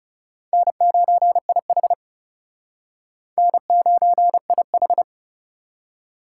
Morse code, heard as a keyed tone, about 34 wpm, 700 Hz. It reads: N9IH N9IH
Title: Speed Racing - USA 1x2 Call Signs 30wpm